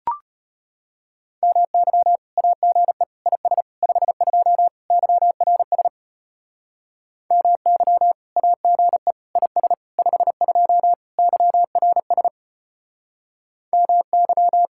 Morse code, heard as a keyed tone, about 32 wpm, 700 Hz. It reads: MY AGE IS 52 YRS MY AGE IS 52 YRS MY